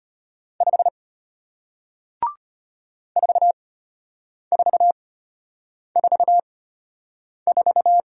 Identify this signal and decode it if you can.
Morse code, heard as a keyed tone, about 33 wpm, spaced as for 8 wpm, 700 Hz. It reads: 5 4444